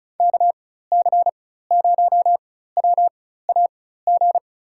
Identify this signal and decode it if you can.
Morse code, heard as a keyed tone, about 36 wpm, spaced as for 16 wpm, 700 Hz. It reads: KC0WAG
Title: USA Call Signs - 6 Characters - 4x Character Spacing - 35wpm